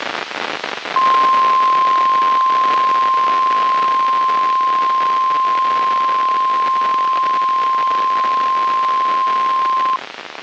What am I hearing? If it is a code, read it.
Emergency Alert System (SAME) tones (the 1050 Hz attention tone), with noise. It is heard from 0.9 to 10.0 s.
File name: eas messages / EAS WXM95 Towanda